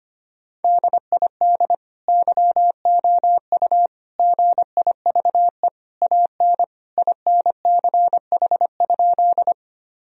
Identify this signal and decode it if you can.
Morse code, heard as a keyed tone, about 25 wpm, 700 Hz. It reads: DID YOU GIVE AN INCH?
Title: Sentences from Top 1000 Words - Review Part 3 - 25wpm